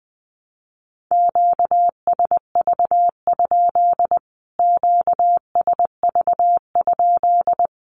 Morse code, heard as a keyed tone, about 20 wpm, 700 Hz. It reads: QSV? QSV?
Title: Q-Codes Repeated 3x 20wpm